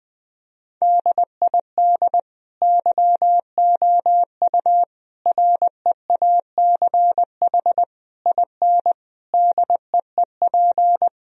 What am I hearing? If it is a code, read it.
Morse code, heard as a keyed tone, about 20 wpm, 700 Hz. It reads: DID YOU REACH IN DEEP
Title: Sentences from Top 1000 Words - Review Part 3 - 20wpm